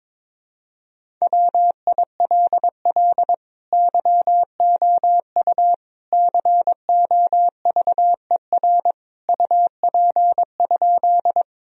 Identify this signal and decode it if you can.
Morse code, heard as a keyed tone, about 22 wpm, 700 Hz. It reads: WILL YOU COVER UP?
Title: Sentences from Top 1000 Words - Review Part 1 - 22wpm